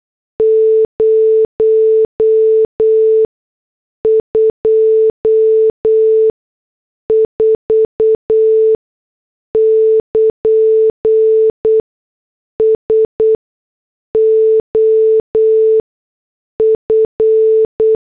Morse code, heard as a keyed tone, about 8 wpm, spaced as for 6 wpm, 440 Hz. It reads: 024 <KN> SOF
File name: morse / headlines